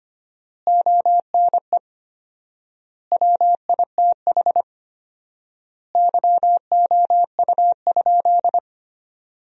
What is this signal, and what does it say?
Morse code, heard as a keyed tone, about 25 wpm, 700 Hz. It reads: ONE WITH YOU?
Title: 4x Word-Spacing - Sentences from Top 500 Words - 25wpm